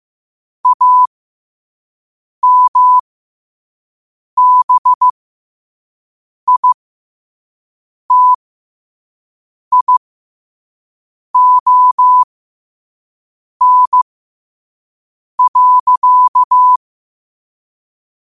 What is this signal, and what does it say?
Morse code, heard as a keyed tone, about 15 wpm, spaced as for 5 wpm, 1000 Hz. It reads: AMBITION.
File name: qotdfiles / 5wpm